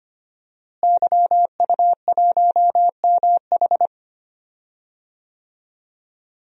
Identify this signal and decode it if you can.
Morse code, heard as a keyed tone, about 25 wpm, 700 Hz. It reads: YU1MH
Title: DXCC-100 Call Signs - 5 Characters - Repeated 3x - 25wpm